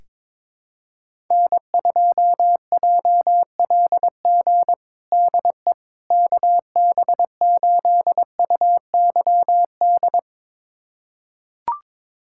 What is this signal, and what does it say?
Morse code, heard as a keyed tone, about 22 wpm, 700 Hz. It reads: N2JLG DE KB8UYD
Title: USA Callsign DE Callsign 22wpm